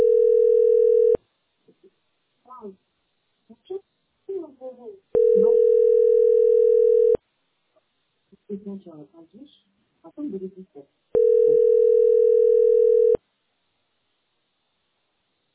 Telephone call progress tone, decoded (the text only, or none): ringback tone